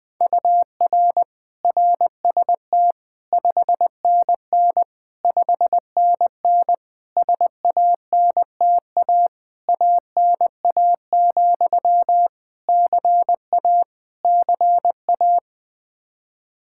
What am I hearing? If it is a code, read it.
Morse code, heard as a keyed tone, about 20 wpm, 700 Hz. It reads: UR RST 5NN 5NN SANTA ANA, CA CA
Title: Practice QSO - Exchange 1 - Beginning - 20wpm